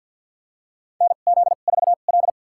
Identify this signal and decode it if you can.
Morse code, heard as a keyed tone, about 52 wpm, spaced as for 36 wpm, 700 Hz. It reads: NZ4L
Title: USA Call Signs - 2x Character Spacing - 50wpm